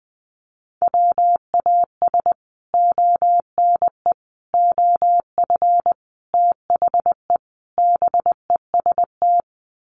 Morse code, heard as a keyed tone, about 20 wpm, 700 Hz. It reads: WAS ONE OF THE BEST
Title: Sets of 5 Words Encore2 20wpm